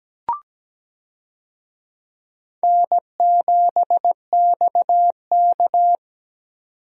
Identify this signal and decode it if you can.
Morse code, heard as a keyed tone, about 17 wpm, 700 Hz. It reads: N7XK
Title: USA Call Signs - 4 Characters - 17wpm